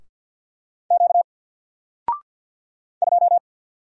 Morse code, heard as a keyed tone, about 49 wpm, 700 Hz. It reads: Q 2
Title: Single Letter-Number 50wpm